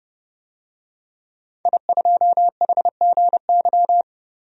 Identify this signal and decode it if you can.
Morse code, heard as a keyed tone, about 31 wpm, 700 Hz. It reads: I2HGY
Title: DXCC-100 Call Signs - 5 Characters - 30wpm